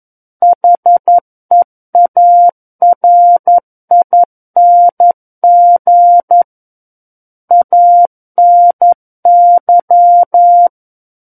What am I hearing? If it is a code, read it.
Morse code, heard as a keyed tone, about 11 wpm, 700 Hz. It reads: HEARING ANY
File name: Book One / 02-Chapter 2 - 11 WPM